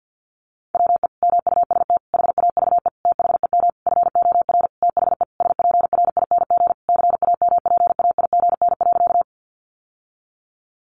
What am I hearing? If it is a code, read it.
Morse code, heard as a keyed tone, about 50 wpm, 700 Hz. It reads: WE MUST HAVE THEM FOR THE SPRING CAMPAIGN.
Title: Sentences from Top 1200 Words 50wpm